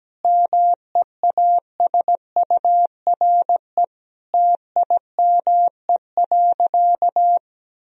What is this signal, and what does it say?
Morse code, heard as a keyed tone, about 17 wpm, 700 Hz. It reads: MEASURE TIME.
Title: Sentences from Top 1000 Words - Review Part 2 - 17wpm